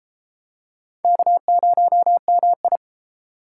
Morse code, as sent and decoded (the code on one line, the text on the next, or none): -.- ----- -- ..
K0MI